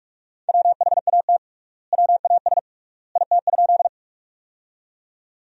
Morse code, heard as a keyed tone, about 46 wpm, 700 Hz. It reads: WHAT WAS IT?
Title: 3x Word-Spacing - Sentences from Top 100 Words - 45wpm